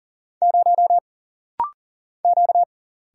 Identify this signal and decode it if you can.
Morse code, heard as a keyed tone, about 40 wpm, 700 Hz. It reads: 0 Q
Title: Single Letter-Number - Rapid-Fire 40wpm